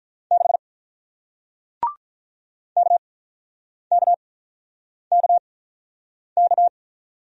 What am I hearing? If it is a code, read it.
Morse code, heard as a keyed tone, about 47 wpm, spaced as for 9 wpm, 700 Hz. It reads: 6 KKKK